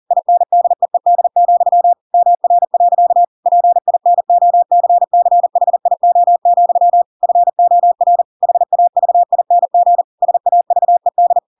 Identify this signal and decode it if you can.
Morse code, heard as a keyed tone, about 40 wpm, 700 Hz. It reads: INDEED, MR. PINOCCHIO, FOR HAVING SAVED
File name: Pinocchio / Pinocchio ch5 40 40 ik1hge